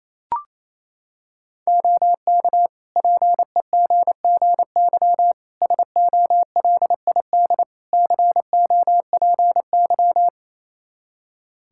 Morse code, heard as a keyed tone, about 28 wpm, 700 Hz. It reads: OK PEGGY SOLID COPY
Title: Practice QSO - Exchange 3 - More Info - 28wpm